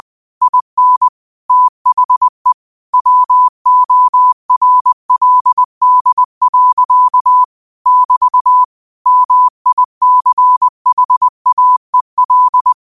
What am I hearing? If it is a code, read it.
Morse code, heard as a keyed tone, about 20 wpm, 1000 Hz. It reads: IN THE WORLD. <BT> MICHAEL
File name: qotdfiles / 20wpm